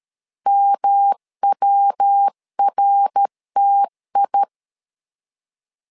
Morse code, heard as a keyed tone, about 13 wpm, 800 Hz. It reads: MWRTI
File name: LEZIONI / 3 MWRO / 3 MWR 12wpm 1711